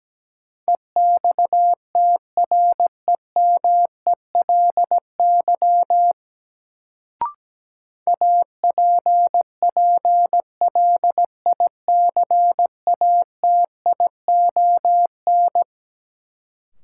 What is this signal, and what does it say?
Morse code, heard as a keyed tone, about 17 wpm, 700 Hz. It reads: EXTREMELY APPLICATION
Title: Sentences from Top 1200 Words - New Word Drill Alt - 17wpm